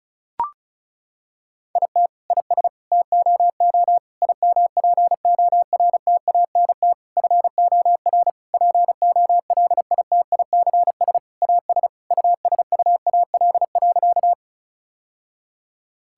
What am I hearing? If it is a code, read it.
Morse code, heard as a keyed tone, about 34 wpm, 700 Hz. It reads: IT IS TOO IMPORTANT FOR POLITICS AS USUAL.
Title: Sentences from Top 1000 Words - Review Part 6 - 35wpm